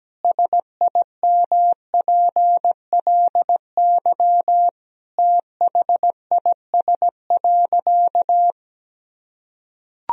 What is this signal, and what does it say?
Morse code, heard as a keyed tone, about 17 wpm, 700 Hz. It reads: SIMPLY THIS.
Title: Sentences from Top 1000 Words - Review Part 4 - 17wpm